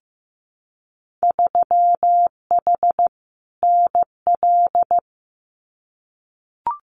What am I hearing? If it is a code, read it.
Morse code, heard as a keyed tone, about 15 wpm, 700 Hz. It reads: 3H NL
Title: Winter Field Day Exchanges Encore 15wpm